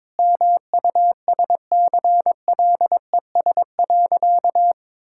Morse code, heard as a keyed tone, about 22 wpm, 700 Hz. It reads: MUSCLES.
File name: practice / 22wpm / Easy Sentences with Hints - 22wpm